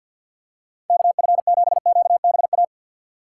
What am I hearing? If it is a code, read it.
Morse code, heard as a keyed tone, about 48 wpm, 700 Hz. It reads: KF7QBA
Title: USA Call Signs - 6 Characters - 6x Character Spacing - 50wpm